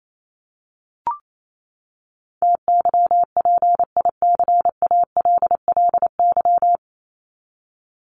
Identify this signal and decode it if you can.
Morse code, heard as a keyed tone, about 28 wpm, 700 Hz. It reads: TYPICALLY
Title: LY Suffix - Top 100 Words with Base Words 28wpm